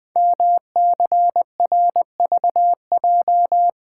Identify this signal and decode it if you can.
Morse code, heard as a keyed tone, about 20 wpm, 700 Hz. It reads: MCRVJ
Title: Code Groups - 5 Letters - 20wpm